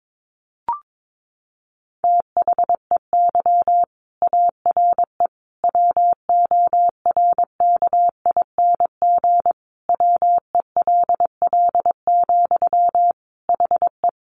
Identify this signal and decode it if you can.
Morse code, heard as a keyed tone, about 22 wpm, 700 Hz. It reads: THEY ARE WORKING WELL, HE
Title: Sentences from Top 700 Words 22wpm